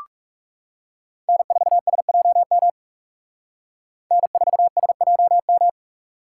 Morse code, heard as a keyed tone, about 38 wpm, 700 Hz. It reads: N4SJM N4SJM